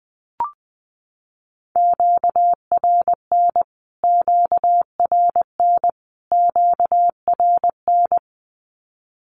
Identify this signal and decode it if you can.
Morse code, heard as a keyed tone, about 20 wpm, 700 Hz. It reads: QRN QRN QRN
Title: Q-Codes Repeated 3x 20wpm